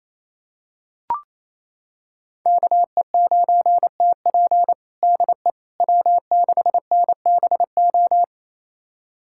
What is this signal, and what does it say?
Morse code, heard as a keyed tone, about 28 wpm, 700 Hz. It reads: KE9TP DE W6NBO
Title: USA Callsign DE Callsign - 5 Characters - 28wpm